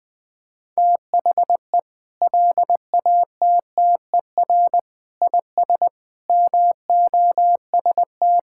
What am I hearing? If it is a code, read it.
Morse code, heard as a keyed tone, about 20 wpm, 700 Hz. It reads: THE LATTER IS MOST
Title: Sentences from Top 1600 Words 20wpm